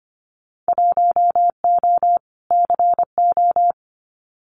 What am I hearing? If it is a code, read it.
Morse code, heard as a keyed tone, about 25 wpm, 700 Hz. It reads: 1O CO
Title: Winter Field Day Exchanges 25wpm